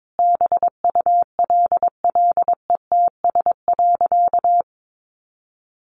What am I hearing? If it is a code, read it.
Morse code, heard as a keyed tone, about 22 wpm, 700 Hz. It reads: BULLETS.
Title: A Sound Of Thunder 22wpm